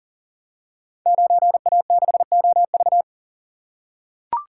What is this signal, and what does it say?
Morse code, heard as a keyed tone, about 42 wpm, 690 Hz. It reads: 9A6OV
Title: Code Groups - 5 Letter:Numbers - 40wpm